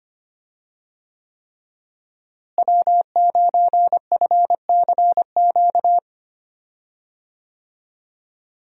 Morse code, heard as a keyed tone, about 25 wpm, 700 Hz. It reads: W9FCQ
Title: USA Call Signs - 5 Characters - Repeated 3x - 25wpm